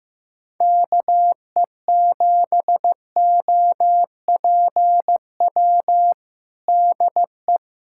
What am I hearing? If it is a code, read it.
Morse code, heard as a keyed tone, about 15 wpm, 700 Hz. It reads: KE7OPW DE
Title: USA Callsign DE Callsign - 6 Characters - 15wpm